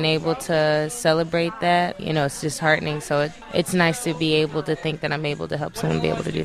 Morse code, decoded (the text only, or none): none